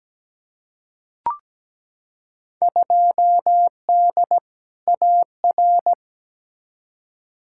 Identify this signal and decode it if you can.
Morse code, heard as a keyed tone, about 17 wpm, 700 Hz. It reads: 2D AR
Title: Field Day Exchanges Encore - 17wpm